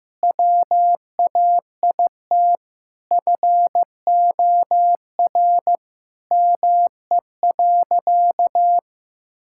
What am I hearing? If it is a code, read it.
Morse code, heard as a keyed tone, about 16 wpm, 700 Hz. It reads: WAIT FOR ME.
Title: Sentences from Top 1000 Words - Review Part 6 - 15wpm